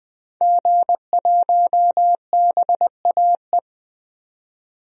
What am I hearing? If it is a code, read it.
Morse code, heard as a keyed tone, about 21 wpm, 700 Hz. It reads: G1BAE
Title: DXCC-100 Call Signs - 5 Characters - 20wpm